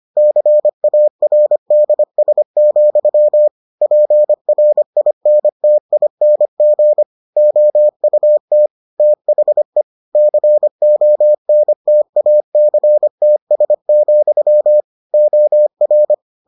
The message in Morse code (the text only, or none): CARDS, PRINTING OUT THE CONTACTS, OR